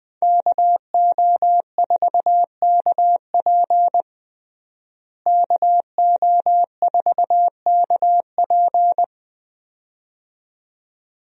Morse code, heard as a keyed tone, about 21 wpm, 700 Hz. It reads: KO4KP KO4KP
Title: USA Call Signs - 5 Characters - Repeated 2x - 20wpm